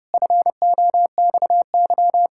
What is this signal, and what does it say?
Morse code, heard as a keyed tone, about 30 wpm, 700 Hz. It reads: FOXY
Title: Pangram Challenge v2 - 30wpm